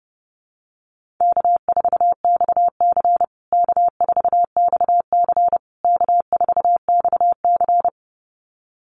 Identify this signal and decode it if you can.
Morse code, heard as a keyed tone, about 30 wpm, 700 Hz. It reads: K4XC K4XC K4XC
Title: USA Call Signs Repeated 3x 30wpm